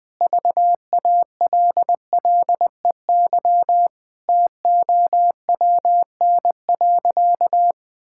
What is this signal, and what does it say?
Morse code, heard as a keyed tone, about 21 wpm, 700 Hz. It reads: VALLEY TOWN.
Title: Pangram Challenge v2 - 20wpm